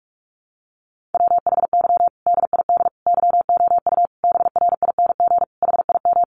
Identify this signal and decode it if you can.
Morse code, heard as a keyed tone, about 44 wpm, 700 Hz. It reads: WHY DID YOU BRING HIM
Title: Sentences from Top 500 Words 45wpm